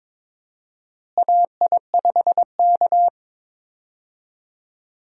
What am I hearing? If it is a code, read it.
Morse code, heard as a keyed tone, about 22 wpm, 700 Hz. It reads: AI5K
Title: USA Call Signs - 4 Characters - Repeated 2x - 22wpm